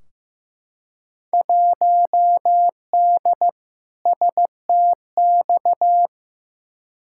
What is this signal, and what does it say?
Morse code, heard as a keyed tone, about 15 wpm, 700 Hz. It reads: 1D STX
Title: Field Day Exchanges - 15wpm